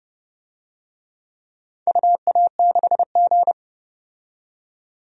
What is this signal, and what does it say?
Morse code, heard as a keyed tone, about 30 wpm, 700 Hz. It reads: UA6G
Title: DXCC-100 Call Signs - 4 Characters - 30wpm